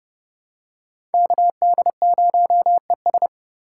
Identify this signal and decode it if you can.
Morse code, heard as a keyed tone, about 30 wpm, 700 Hz. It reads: KD0ES
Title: USA Call Signs - 5 Characters - 2x Character Spacing - 30wpm